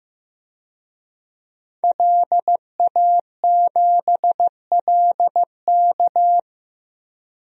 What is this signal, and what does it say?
Morse code, heard as a keyed tone, about 15 wpm, 700 Hz. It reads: LA7LK
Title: DXCC-100 Call Signs - 5 Characters - Repeated 2x - 15wpm